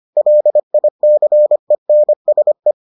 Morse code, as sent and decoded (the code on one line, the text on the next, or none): .-.. .. -.-. -. ... .
LICENSE